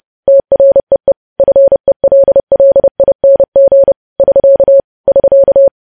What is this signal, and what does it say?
Morse code, heard as a keyed tone, about 30 wpm, 550 Hz. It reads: TREE FELLING <SK> <SK>
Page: news headlines in morse code at 30 wpm